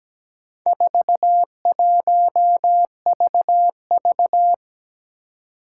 Morse code, heard as a keyed tone, about 17 wpm, 700 Hz. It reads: 41VV
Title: Four Challenging Letter-Number Combinations 17wpm